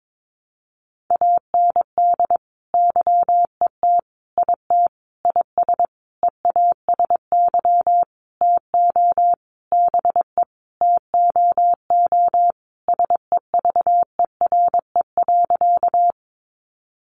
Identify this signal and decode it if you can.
Morse code, heard as a keyed tone, about 22 wpm, 700 Hz. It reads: AND YET IT IS EASY TO BE TOO SEVERE.
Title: Sentences from Top 1400 Words 22wpm